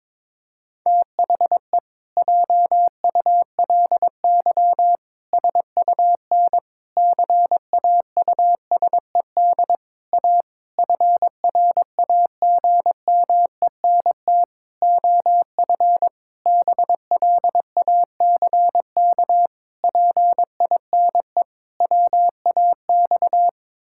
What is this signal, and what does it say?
Morse code, heard as a keyed tone, about 22 wpm, 700 Hz. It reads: THE JULY SUN CAUSED A FRAGMENT OF BLACK PINE WAX
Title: Pangram Challenge v2 - 22wpm